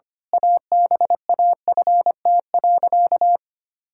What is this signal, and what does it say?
Morse code, heard as a keyed tone, about 25 wpm, 700 Hz. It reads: ABAFT.
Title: Pangram Challenge v2 - 25wpm